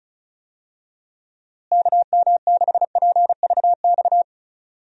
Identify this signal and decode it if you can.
Morse code, heard as a keyed tone, about 34 wpm, 700 Hz. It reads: KM6PVX
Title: USA Call Signs - 6 Characters - Repeated 3x - 35wpm